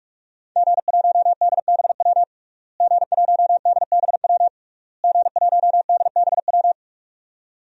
Morse code, heard as a keyed tone, about 44 wpm, 700 Hz. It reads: G1DBW G1DBW G1DBW
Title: DXCC-100 Call Signs - 5 Characters - Repeated 3x - 45wpm